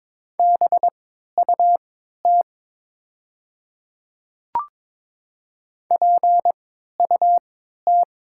Morse code, heard as a keyed tone, about 22 wpm, spaced as for 12 wpm, 700 Hz. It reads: BUT PUT
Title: 3x Character Spacing - Top 500 Words - 22wpm